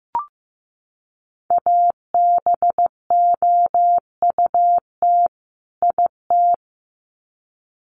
Morse code, heard as a keyed tone, about 15 wpm, 700 Hz. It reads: ABOUT IT
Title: Sets of 2 Words 15wpm